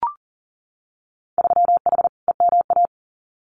Morse code, heard as a keyed tone, about 40 wpm, 700 Hz. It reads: 3H EMA
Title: Winter Field Day Exchanges Encore 40wpm